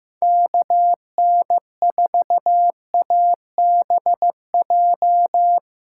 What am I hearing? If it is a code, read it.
Morse code, heard as a keyed tone, about 15 wpm, 700 Hz. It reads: KN4ABJ
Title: USA Call Signs - 6 Characters - 6x Character Spacing - 15wpm